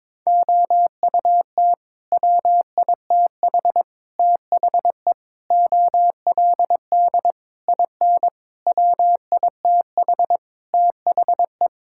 Morse code, heard as a keyed tone, about 21 wpm, 700 Hz. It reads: OUT WITH THE OLD IN WITH THE
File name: practice / 22wpm / Sets of 8 Words - 22wpm